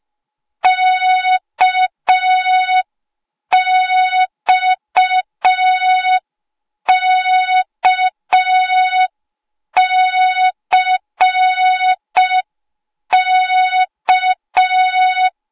Morse code, heard as a keyed tone, about 5 wpm, 3700 Hz. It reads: KXKCK
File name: Morze-HA2MN / hang / 08KXC HA2MN